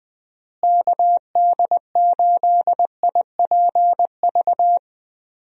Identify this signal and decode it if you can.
Morse code, heard as a keyed tone, about 20 wpm, 700 Hz. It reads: KD8IPV
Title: USA Call Signs - 6 Characters - Repeated 2x - 20wpm